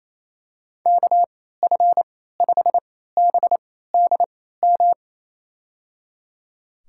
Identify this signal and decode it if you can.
Morse code, heard as a keyed tone, about 27 wpm, spaced as for 16 wpm, 700 Hz. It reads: KF5BDM
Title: USA Call Signs - 6 Characters - 3x Character Spacing - 28wpm